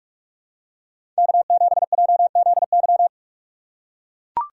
Morse code, heard as a keyed tone, about 44 wpm, 690 Hz. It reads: K7JZY
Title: USA Call Signs - 5 Characters - 6x Character Spacing - 45wpm